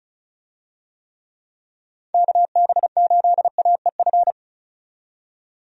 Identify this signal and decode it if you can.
Morse code, heard as a keyed tone, about 34 wpm, 700 Hz. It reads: KB8AEF
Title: USA Call Signs - 6 Characters - 35wpm